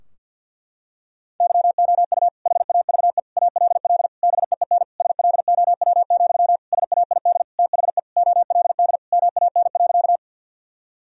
Morse code, heard as a keyed tone, about 51 wpm, 700 Hz. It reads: YOU HAVE ALL BEEN SLOW, SAID THE OLD MAN.
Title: Sentences from Top 1000 Words - Review Part 2 - 50wpm